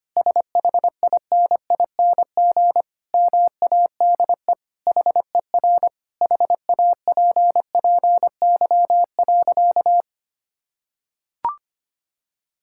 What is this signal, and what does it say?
Morse code, heard as a keyed tone, about 25 wpm, 700 Hz. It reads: SHINING MADE HER HAPPY.
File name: practice / 25wpm / Sentences with 4-Word Phrases - 25wpm